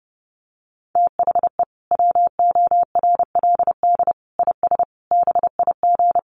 Morse code, heard as a keyed tone, about 28 wpm, 700 Hz. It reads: THE WORLD IS BIG
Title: Sentences from Top 400 Words 30wpm